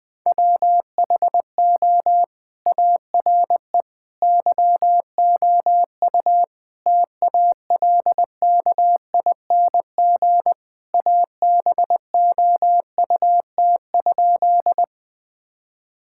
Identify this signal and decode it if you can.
Morse code, heard as a keyed tone, about 20 wpm, 700 Hz. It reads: WHO ARE YOU TALKING ABOUT?
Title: Sentences with 2-Word Phrases - Spoken Beginning - 20wpm